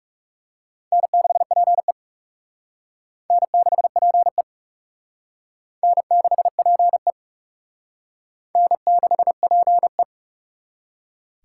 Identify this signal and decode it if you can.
Morse code, heard as a keyed tone, about 38 wpm, 700 Hz. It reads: N6PE N6PE N6PE N6PE